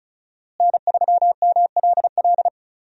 Morse code, heard as a keyed tone, about 36 wpm, 700 Hz. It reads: N3MLL
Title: USA Call Signs - 5 Characters - Repeated 3x - 35wpm